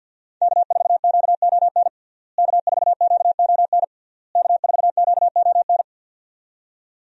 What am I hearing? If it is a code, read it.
Morse code, heard as a keyed tone, about 51 wpm, 700 Hz. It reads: K4QON K4QON K4QON
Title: USA Call Signs - 5 Characters - Repeated 3x - 50wpm